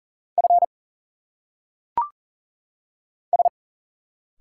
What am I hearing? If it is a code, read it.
Morse code, heard as a keyed tone, about 40 wpm, 700 Hz. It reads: F S
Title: Single Letters - Phonetic Pronunciation 40wpm